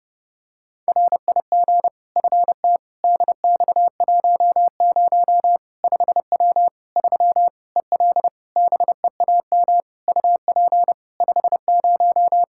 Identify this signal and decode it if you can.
Morse code, heard as a keyed tone, about 32 wpm, 700 Hz. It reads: RIG FT DX10 5W 3 EL BEAM UP 50
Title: Practice QSO - Exchange 2 - Info - 30wpm